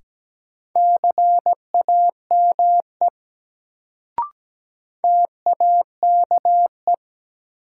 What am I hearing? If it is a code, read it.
Morse code, heard as a keyed tone, about 17 wpm, 700 Hz. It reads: CAME TAKE
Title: Sentences from Top 200 Words - New Word Drill - 17wpm